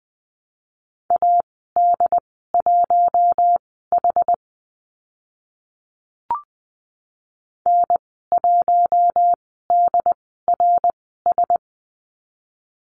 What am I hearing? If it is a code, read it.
Morse code, heard as a keyed tone, about 20 wpm, spaced as for 14 wpm, 700 Hz. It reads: AD1H N1DRS
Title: 069 Lesson 12g - Letter d - Call Signs 20wpm